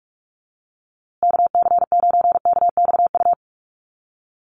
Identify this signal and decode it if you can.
Morse code, heard as a keyed tone, about 44 wpm, 700 Hz. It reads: KC9KXU
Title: USA Call Signs Repeated 3x 45wpm